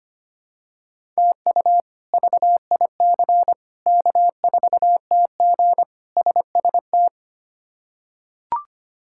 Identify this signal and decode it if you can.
Morse code, heard as a keyed tone, about 25 wpm, 700 Hz. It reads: TU VIC K4TG SST
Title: K1USN SST QSO Practice - 25wpm